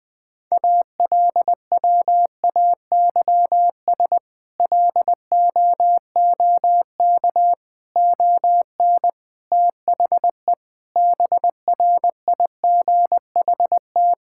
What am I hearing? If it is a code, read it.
Morse code, heard as a keyed tone, about 20 wpm, 700 Hz. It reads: ALWAYS LOOK ON THE BRIGHT
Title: Sets of 8 Words - 20wpm